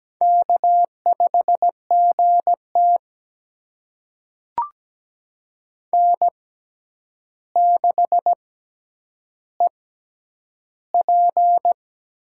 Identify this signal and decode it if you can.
Morse code, heard as a keyed tone, about 17 wpm, spaced as for 9 wpm, 700 Hz. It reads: K5GT N6EP
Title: USA Call Signs - 4 Characters - 6x Character Spacing - 17wpm